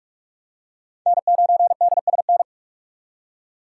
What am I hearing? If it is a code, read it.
Morse code, heard as a keyed tone, about 47 wpm, 700 Hz. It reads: N9DSN
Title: USA Call Signs - 5 Characters - 3x Character Spacing - 45wpm